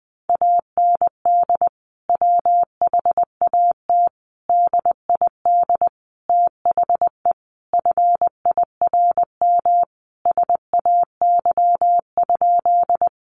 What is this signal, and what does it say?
Morse code, heard as a keyed tone, about 20 wpm, 700 Hz. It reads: AND WHAT DID THE FIRM SAY?
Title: Sentences from Top 1400 Words 20wpm